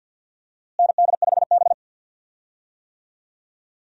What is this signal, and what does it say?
Morse code, heard as a keyed tone, about 53 wpm, 700 Hz. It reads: ND5B
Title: USA Call Signs - 4 Characters - Repeated 3x - 50wpm